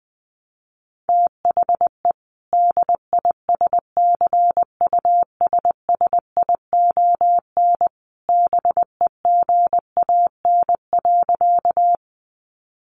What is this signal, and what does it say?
Morse code, heard as a keyed tone, about 19 wpm, 700 Hz. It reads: THE DISCUSSION BEGAN.
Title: Sentences from Top 1100 Words 20wpm